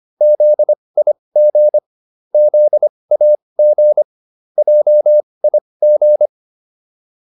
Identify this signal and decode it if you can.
Morse code, heard as a keyed tone, about 25 wpm, spaced as for 20 wpm, 600 Hz. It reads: ZIG ZAG JIG